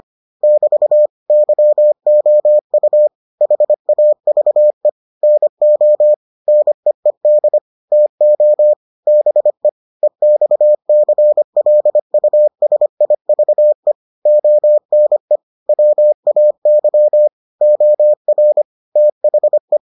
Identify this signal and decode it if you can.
Morse code, heard as a keyed tone, about 25 wpm, 600 Hz. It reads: <BT> YOU HAVE NO NEED TO BE EXCLUSIVE ONE WAY OR THE